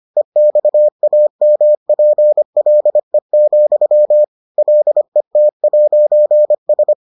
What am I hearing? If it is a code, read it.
Morse code, heard as a keyed tone, about 25 wpm, 600 Hz. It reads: EXAMPLE, LET'S